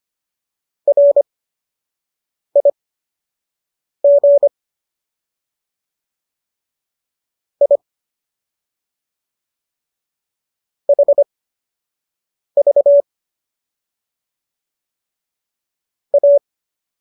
Morse code, heard as a keyed tone, about 25 wpm, spaced as for 6 wpm, 570 Hz. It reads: RIG I HV A